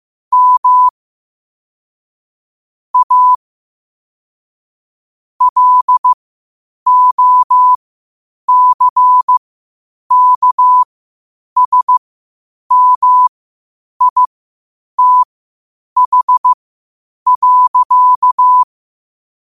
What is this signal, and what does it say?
Morse code, heard as a keyed tone, about 15 wpm, spaced as for 8 wpm, 1000 Hz. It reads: M A LOCKSMITH.
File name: qotdfiles / 8wpm